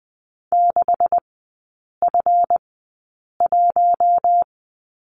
Morse code, heard as a keyed tone, about 20 wpm, spaced as for 8 wpm, 700 Hz. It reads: 6F1
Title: 205 Lesson 33e - Letter j - Three Characters 20wpm